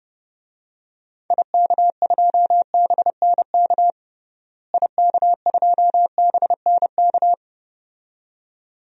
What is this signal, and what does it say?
Morse code, heard as a keyed tone, about 31 wpm, 700 Hz. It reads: IK2BNK IK2BNK
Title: DXCC-100 Call Signs - 6 Characters - Repeated 2x - 30wpm